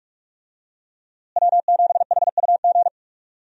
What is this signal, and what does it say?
Morse code, heard as a keyed tone, about 43 wpm, 700 Hz. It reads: W7HUG
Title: USA Call Signs - 5 Characters - Repeated 2x - 45wpm